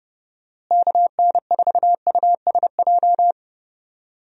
Morse code, heard as a keyed tone, about 30 wpm, 700 Hz. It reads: KN4USJ
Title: USA Call Signs - 6 Characters - Repeated 3x - 30wpm